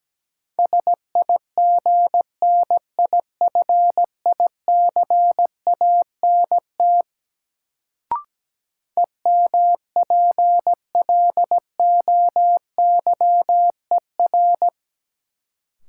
Morse code, heard as a keyed tone, about 17 wpm, 700 Hz. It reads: SIGNIFICANT EMPLOYER
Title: Sentences from Top 1600 Words - New Word Drill Alt - 17wpm